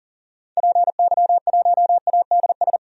Morse code, heard as a keyed tone, about 38 wpm, 700 Hz. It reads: PY1ADS
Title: DXCC-100 Call Signs - 6 Characters - 40wpm